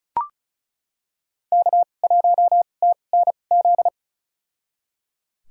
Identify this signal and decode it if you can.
Morse code, heard as a keyed tone, about 36 wpm, spaced as for 25 wpm, 700 Hz. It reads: K1TNZ
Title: USA Call Signs - 2x Character Spacing - 35wpm